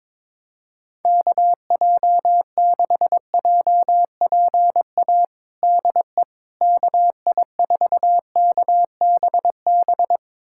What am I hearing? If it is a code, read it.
Morse code, heard as a keyed tone, about 22 wpm, 700 Hz. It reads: KJ6JPA DE KI4KBB
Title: USA Callsign DE Callsign - 6 Characters - 22wpm